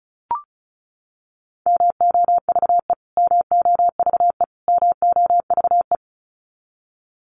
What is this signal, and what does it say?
Morse code, heard as a keyed tone, about 34 wpm, 700 Hz. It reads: MOVE MOVE MOVE
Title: Top 200 Words Repeated 35wpm